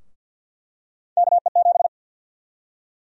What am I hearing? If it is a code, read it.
Morse code, heard as a keyed tone, about 52 wpm, 700 Hz. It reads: KE7
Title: USA Call Sign Prefixes - 50wpm